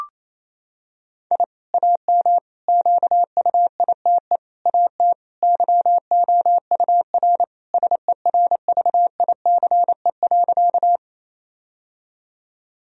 Morse code, heard as a keyed tone, about 27 wpm, 700 Hz. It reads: I AM QUITE AT YOUR SERVICE.
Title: Sentences from Top 1000 Words - Review Part 4 - 28wpm